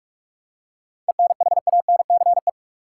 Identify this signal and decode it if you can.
Morse code, heard as a keyed tone, about 46 wpm, 700 Hz. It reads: ENHANCE
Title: Seven Letter Words - 45wpm